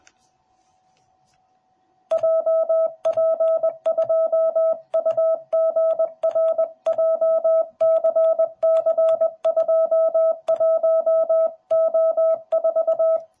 Morse code, heard as a keyed tone, about 20 wpm, 650 Hz. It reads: JP2UGR JCC21O4